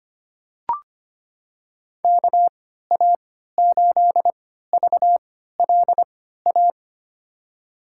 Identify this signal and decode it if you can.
Morse code, heard as a keyed tone, about 25 wpm, spaced as for 14 wpm, 700 Hz. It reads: KA8VLA
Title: USA Call Signs - 6 Characters - 3x Character Spacing - 25wpm